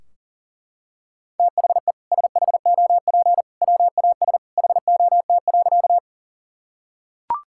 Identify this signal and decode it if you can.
Morse code, heard as a keyed tone, about 40 wpm, 700 Hz. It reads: THE SHOP WAS HOT.
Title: Sentences from Top 1600 Words 40wpm